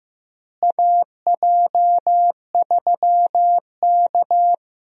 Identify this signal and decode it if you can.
Morse code, heard as a keyed tone, about 15 wpm, 700 Hz. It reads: AJ3K
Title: USA Call Signs - 4 Characters - Repeated 3x - 15wpm